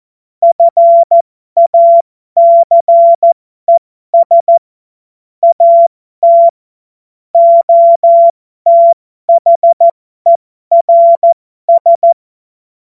Morse code, heard as a keyed tone, about 14 wpm, spaced as for 12 wpm, 680 Hz. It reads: FACES AT OTHERS
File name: Morsekurs / Witze